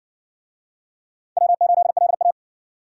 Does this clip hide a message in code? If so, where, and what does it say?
Yes: Morse code from 1.4 to 2.3 s, heard as a keyed tone, about 60 wpm, 700 Hz. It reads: W8LA